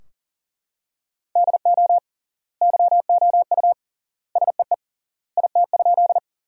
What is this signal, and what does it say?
Morse code, heard as a keyed tone, about 37 wpm, 700 Hz. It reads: DO YOU SEE IT?
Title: 3x Word-Spacing - Sentences from Top 100 Words - 40wpm